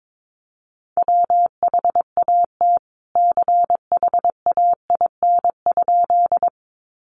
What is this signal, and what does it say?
Morse code, heard as a keyed tone, about 22 wpm, 700 Hz. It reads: WHAT CHAIN?
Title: Sentences from Top 1500 Words 22wpm